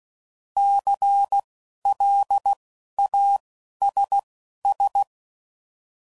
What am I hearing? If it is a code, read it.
Morse code, heard as a keyed tone, about 16 wpm, spaced as for 11 wpm, 800 Hz. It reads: CLASS
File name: CW Code Groups @10WPM / QSOs / QSO kc7sac